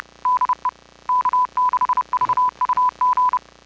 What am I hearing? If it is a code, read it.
Morse code, heard as a keyed tone, about 30 wpm, 1000 Hz. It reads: DE K6VUG